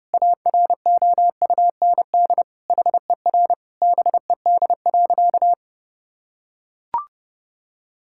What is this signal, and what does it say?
Morse code, heard as a keyed tone, about 30 wpm, 700 Hz. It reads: AROUND HER BED.
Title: Sentences from Top 1000 Words - Review Part 6 - 30wpm